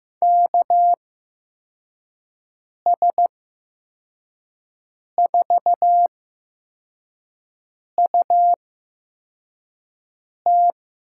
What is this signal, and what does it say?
Morse code, heard as a keyed tone, about 15 wpm, spaced as for 4 wpm, 700 Hz. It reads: KS4UT
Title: USA Call Signs - 5 Characters - 8x Character Spacing - 15wpm